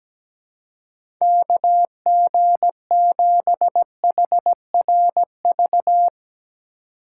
Morse code, heard as a keyed tone, about 17 wpm, 700 Hz. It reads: KG7HRV
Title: USA Call Signs - 2x Character Spacing - 17wpm